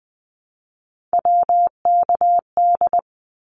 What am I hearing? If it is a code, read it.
Morse code, heard as a keyed tone, about 20 wpm, 700 Hz. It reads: WKD
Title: Abbreviations 20wpm